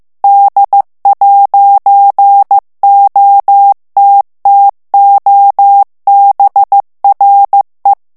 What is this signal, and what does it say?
Morse code, heard as a keyed tone, about 15 wpm, 800 Hz. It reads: D'OTTOBRE
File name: morse mp3 / ITALIEN texte clair 1 vitesse 14 mots minute